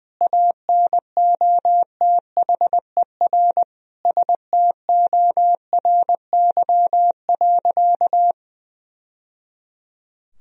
Morse code, heard as a keyed tone, about 21 wpm, 700 Hz. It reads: ANOTHER STORY.
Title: Sentences from Top 1000 Words - Review Part 1 - 20wpm